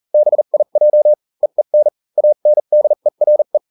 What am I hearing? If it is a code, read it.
Morse code, heard as a keyed tone, about 40 wpm, spaced as for 35 wpm, 600 Hz. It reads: BIJ EEN ANDERE